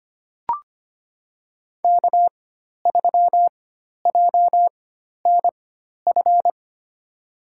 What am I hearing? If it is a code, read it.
Morse code, heard as a keyed tone, about 25 wpm, spaced as for 12 wpm, 700 Hz. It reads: K3JNF
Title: USA Call Signs - 4x Character Spacing - 25wpm